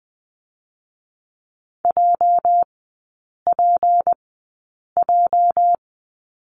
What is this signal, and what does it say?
Morse code, heard as a keyed tone, about 19 wpm, spaced as for 8 wpm, 700 Hz. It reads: JPJ
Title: 205 Lesson 33e - Letter j - Three Characters 20wpm